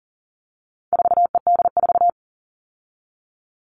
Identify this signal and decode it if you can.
Morse code, heard as a keyed tone, about 40 wpm, 700 Hz. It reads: 4ED4
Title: All Cut Numbers - 4-Digit 40wpm